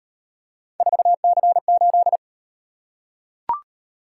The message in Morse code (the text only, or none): 4C8